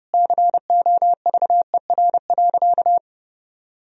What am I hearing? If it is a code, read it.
Morse code, heard as a keyed tone, about 30 wpm, 700 Hz. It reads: COVER.
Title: Sentences from Top 1000 Words - Review Part 1 - 30wpm